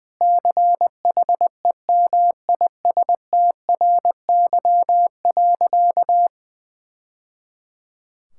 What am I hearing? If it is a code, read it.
Morse code, heard as a keyed tone, about 20 wpm, 700 Hz. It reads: CHEMISTRY.
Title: Sentences from Top 1000 Words - Review Part 6 - 20wpm